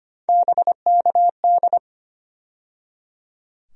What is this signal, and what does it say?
Morse code, heard as a keyed tone, about 25 wpm, 700 Hz. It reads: BKD
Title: USA Call Sign Postfixes - 3 Letters 25wpm